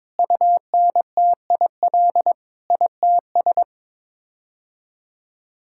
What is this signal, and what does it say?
Morse code, heard as a keyed tone, about 22 wpm, 700 Hz. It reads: UNTIL ITS
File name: practice / 22wpm / Sets of 8 Words - 22wpm